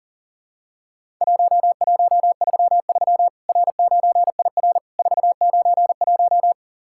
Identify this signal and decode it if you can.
Morse code, heard as a keyed tone, about 40 wpm, 700 Hz. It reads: 1133 R9IR 491
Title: Worked All Europe DX Contest - QTC Exchange - 40wpm